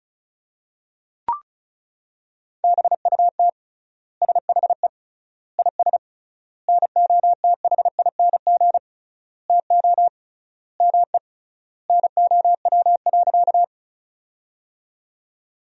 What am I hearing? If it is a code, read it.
Morse code, heard as a keyed tone, about 34 wpm, 700 Hz. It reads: BUT SHE IS NOTHING TO ME NOW.
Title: 3x Word-Spacing - Sentences from Top 500 Words - 35wpm